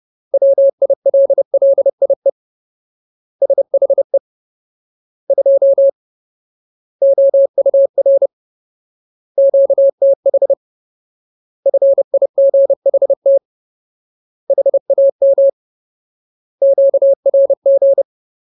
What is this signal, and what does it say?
Morse code, heard as a keyed tone, about 30 wpm, 550 Hz. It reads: WILLIE SHE 2 OUR QTH FIGHT HAM QRG
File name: Ultimate Word List / 30WPM Ultimate